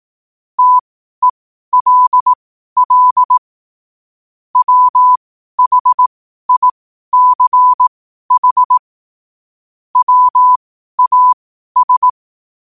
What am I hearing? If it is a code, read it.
Morse code, heard as a keyed tone, about 18 wpm, spaced as for 12 wpm, 1000 Hz. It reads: TELL WHICH WAS
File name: qotdfiles / 12wpm